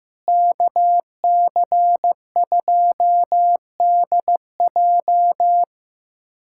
Morse code, heard as a keyed tone, about 15 wpm, 700 Hz. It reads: KC2DJ
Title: USA Callsign DE Callsign - 5 Characters - 15wpm